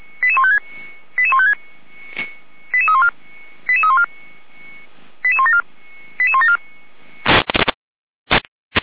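Five-tone selective call (ZVEI), heard as quick runs of five tones, tones of about 80 ms. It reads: 88156 88156 88314 88314 88174 88174
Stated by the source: ZVEI five-tone selective call received on SDR, logged by Feuerwehr Orlamünde (f85360).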